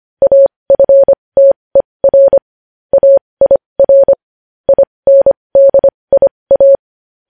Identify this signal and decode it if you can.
Morse code, heard as a keyed tone, about 25 wpm, spaced as for 20 wpm, 550 Hz. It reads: AFTER AIR INDIA
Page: News headlines in Morse code at 20 wpm